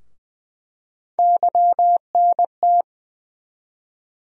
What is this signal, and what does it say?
Morse code, heard as a keyed tone, about 20 wpm, 700 Hz. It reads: YNT
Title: USA Call Sign Postfixes - 3 Letters 20wpm